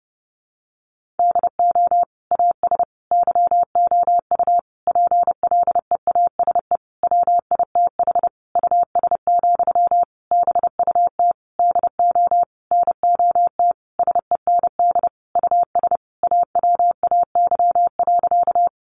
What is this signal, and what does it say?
Morse code, heard as a keyed tone, about 30 wpm, 700 Hz. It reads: DO AS YOU PLEASE WITH US, BUT DO NOT SEND US AWAY.
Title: Sentences from Top 1500 Words 30wpm